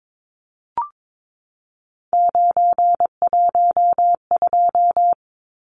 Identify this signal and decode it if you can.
Morse code, heard as a keyed tone, about 22 wpm, 700 Hz. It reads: 912
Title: Numbers - 3-Digit 22wpm